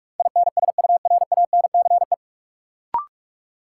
Morse code, heard as a keyed tone, about 46 wpm, 700 Hz. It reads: INSURANCE
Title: Sentences from Top 1100 Words - New Word Drill Alt - 45wpm